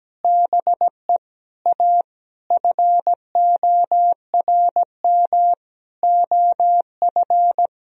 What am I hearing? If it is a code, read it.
Morse code, heard as a keyed tone, about 17 wpm, 700 Hz. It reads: BE A FORM OF
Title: Sentences from Top 1000 Words - Review Part 1 - 17wpm